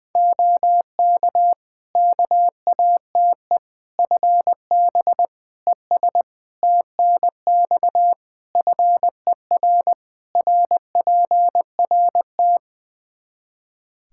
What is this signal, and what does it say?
Morse code, heard as a keyed tone, about 20 wpm, 700 Hz. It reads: OK KATE FB ES TNX FER RPRT
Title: Practice QSO - Exchange 2 - Info - 20wpm